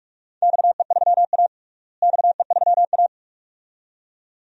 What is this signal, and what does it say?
Morse code, heard as a keyed tone, about 44 wpm, 700 Hz. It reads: XE3A XE3A
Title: DXCC-100 Call Signs - 4 Characters - Repeated 2x - 45wpm